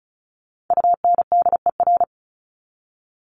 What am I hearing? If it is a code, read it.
Morse code, heard as a keyed tone, about 34 wpm, 700 Hz. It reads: UNDER
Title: Top 300 Words 35wpm